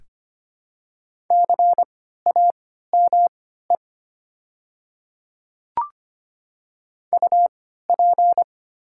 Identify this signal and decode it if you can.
Morse code, heard as a keyed tone, about 25 wpm, spaced as for 14 wpm, 700 Hz. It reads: CAME UP
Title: 3x Character Spacing - Top 500 Words - 25wpm